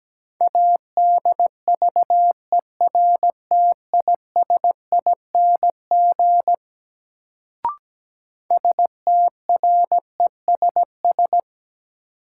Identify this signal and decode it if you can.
Morse code, heard as a keyed tone, about 17 wpm, 700 Hz. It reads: ADVERTISING STRESS
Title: Sentences from Top 1000 Words - Review Part 6 - New Word Drill - 17wpm